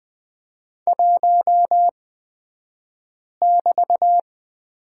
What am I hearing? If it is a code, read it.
Morse code, heard as a keyed tone, about 20 wpm, spaced as for 5 wpm, 700 Hz. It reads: 1<BT>
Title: 223 Lesson 36d - BT - Two Characters 20wpm